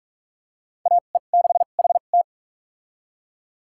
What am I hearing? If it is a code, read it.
Morse code, heard as a keyed tone, about 41 wpm, spaced as for 31 wpm, 700 Hz. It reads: AE6HT
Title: USA Call Signs - 5 Characters - 2x Character Spacing - 45wpm